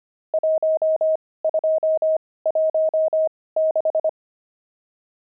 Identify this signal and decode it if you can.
Morse code, heard as a keyed tone, about 25 wpm, spaced as for 18 wpm, 630 Hz. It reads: 1216
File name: pangram-fs-18-cs-25-nums1